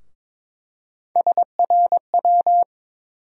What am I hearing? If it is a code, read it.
Morse code, heard as a keyed tone, about 21 wpm, 700 Hz. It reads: SRW
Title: USA Call Sign Postfixes - 3 Letters 22wpm